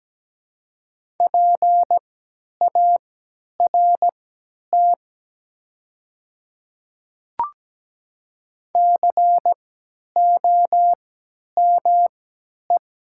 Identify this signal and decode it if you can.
Morse code, heard as a keyed tone, about 17 wpm, spaced as for 10 wpm, 700 Hz. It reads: PART COME
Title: 3x Character Spacing - Top 200 Words - 17wpm